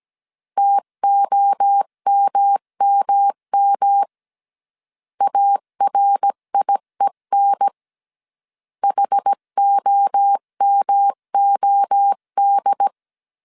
Morse code, heard as a keyed tone, about 17 wpm, 800 Hz. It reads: TOMMM ARIEN HOMOD
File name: CW / DET / 1 3 1500C S A 16wpm 2141